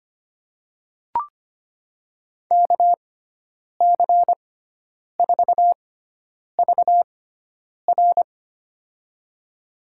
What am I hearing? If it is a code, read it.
Morse code, heard as a keyed tone, about 25 wpm, spaced as for 9 wpm, 700 Hz. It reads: KC4VR
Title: USA Call Signs - 5 Characters - 6x Character Spacing - 25wpm